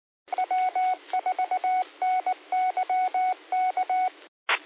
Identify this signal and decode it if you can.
Morse code, heard as a keyed tone, about 19 wpm, 740 Hz. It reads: W4NYK
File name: qsorec SimplexLogic